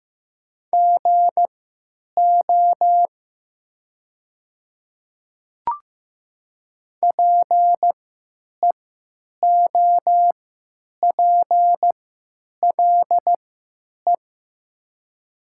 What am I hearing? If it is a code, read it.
Morse code, heard as a keyed tone, about 15 wpm, spaced as for 9 wpm, 700 Hz. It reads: GO PEOPLE
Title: 3x Character Spacing - Top 100 Words - 15wpm